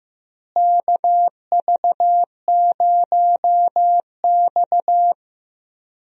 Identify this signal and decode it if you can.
Morse code, heard as a keyed tone, about 15 wpm, 700 Hz. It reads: KV0X
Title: USA Call Signs - 4 Characters - Repeated 2x - 15wpm